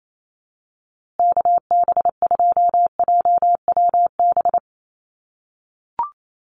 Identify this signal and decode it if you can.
Morse code, heard as a keyed tone, about 28 wpm, 700 Hz. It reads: KB2JWB
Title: USA Call Signs Repeated 3x 28wpm